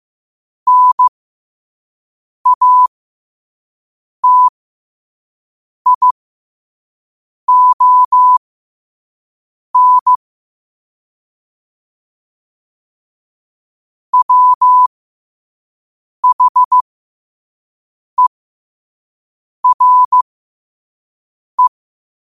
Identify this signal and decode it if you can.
Morse code, heard as a keyed tone, about 15 wpm, spaced as for 5 wpm, 1000 Hz. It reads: NATION WHERE
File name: qotdfiles / 5wpm